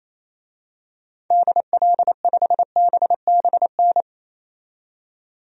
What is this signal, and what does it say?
Morse code, heard as a keyed tone, about 28 wpm, 700 Hz. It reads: DL5BBN
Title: DXCC-100 Call Signs - 6 Characters - 28wpm